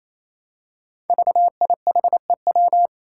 Morse code, heard as a keyed tone, about 29 wpm, 700 Hz. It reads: VIHEW
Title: Code Groups - 5 Letter:Numbers - 28wpm